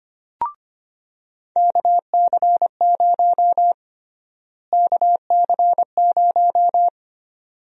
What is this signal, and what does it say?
Morse code, heard as a keyed tone, about 25 wpm, 700 Hz. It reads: KC0 KC0
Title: USA Call Sign Prefixes - Repeated 2x - 25wpm